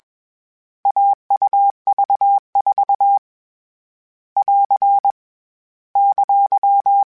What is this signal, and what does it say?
Morse code, heard as a keyed tone, about 21 wpm, 800 Hz. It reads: AUV4 <AR> !
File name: Short Runs / Short Runs-mp3s / AUV AUV4 @20